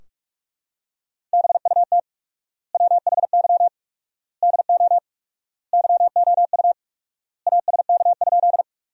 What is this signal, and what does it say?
Morse code, heard as a keyed tone, about 46 wpm, 700 Hz. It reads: BUT WHY DO YOU ASK?
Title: 4x Word-Spacing - Sentences from Top 200 Words - 45wpm